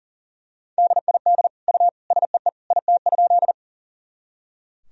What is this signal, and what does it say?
Morse code, heard as a keyed tone, about 40 wpm, 700 Hz. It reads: DID U SEE IT?
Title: Sentences from Top 100 Words and Abbreviations 40wpm